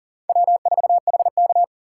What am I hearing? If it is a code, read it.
Morse code, heard as a keyed tone, about 40 wpm, 700 Hz. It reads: W4HK
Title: USA Call Signs - 3x Character Spacing - 40wpm